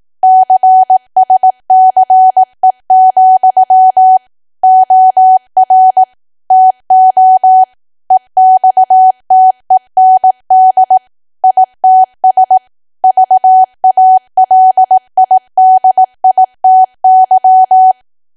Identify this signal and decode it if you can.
Morse code, heard as a keyed tone, about 18 wpm, 750 Hz. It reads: CSCE, OR TO EXTEND ITS VALIDITY
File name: cwa / W1AW-Level2 / 18WPM